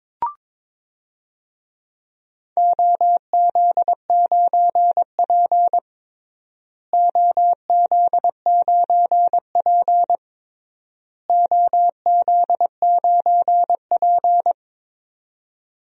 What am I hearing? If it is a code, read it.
Morse code, heard as a keyed tone, about 22 wpm, 700 Hz. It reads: OZ9P OZ9P OZ9P
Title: DXCC-100 Call Signs - 4 Characters - Repeated 3x - 22wpm